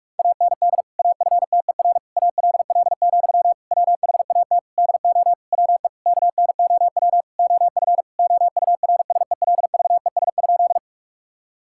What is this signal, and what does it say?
Morse code, heard as a keyed tone, about 46 wpm, 700 Hz. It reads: AND AFTER ALL, WHAT DO WE KNOW OF OURSELVES?
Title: Sentences from Top 1600 Words 45wpm